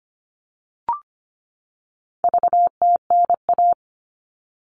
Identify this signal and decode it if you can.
Morse code, heard as a keyed tone, about 25 wpm, 700 Hz. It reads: VTNA
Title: All Cut Numbers - 4-Digit 25wpm